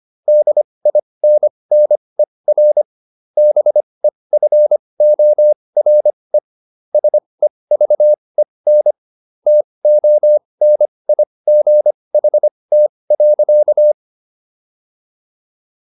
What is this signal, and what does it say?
Morse code, heard as a keyed tone, about 25 wpm, spaced as for 20 wpm, 600 Hz. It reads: DINNER BEFORE SEVEN TONIGHT.